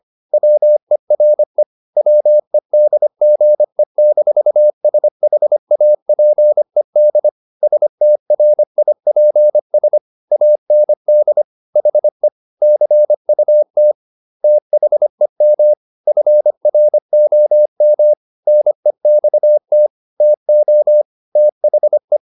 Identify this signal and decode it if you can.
Morse code, heard as a keyed tone, about 25 wpm, 600 Hz. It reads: WERE WEDGE-SHAPED STRIPS AND HE CUT THEM FROM NEXT TO THE
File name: the old man and the sea part 1 25wpm